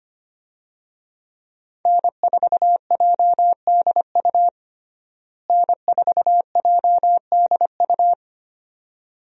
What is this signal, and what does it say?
Morse code, heard as a keyed tone, about 25 wpm, 700 Hz. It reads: N4JDU N4JDU
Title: USA Call Signs - 5 Characters - Repeated 2x - 25wpm